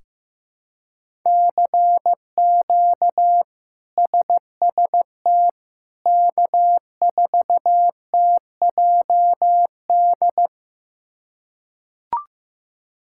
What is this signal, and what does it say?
Morse code, heard as a keyed tone, about 15 wpm, 700 Hz. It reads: CQ SST K4TJD
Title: K1USN SST QSO Practice - 15wpm